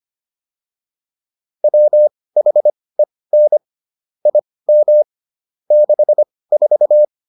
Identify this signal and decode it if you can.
Morse code, heard as a keyed tone, about 25 wpm, spaced as for 18 wpm, 600 Hz. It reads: WHEN IM 64
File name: PR305 18